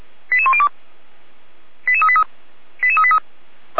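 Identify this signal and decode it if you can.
Five-tone selective call (ZVEI), heard as quick runs of five tones, tones of about 70 ms. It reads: 88102 88383 88383
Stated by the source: ZVEI five-tone call 88192 received by SDR, Feuerwehr Orlamünde.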